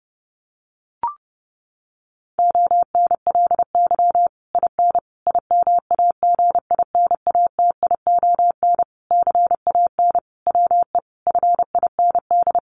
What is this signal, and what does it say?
Morse code, heard as a keyed tone, about 30 wpm, 700 Hz. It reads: ONLY IN IMAGINATION CAN WE FIND
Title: Sentences from Top 1300 Words 30wpm